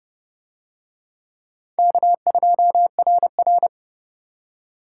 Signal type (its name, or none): Morse code